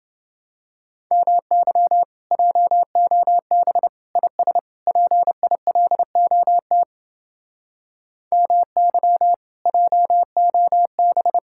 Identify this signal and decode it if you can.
Morse code, heard as a keyed tone, about 30 wpm, 700 Hz. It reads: MY JOB IS PILOT MY JOB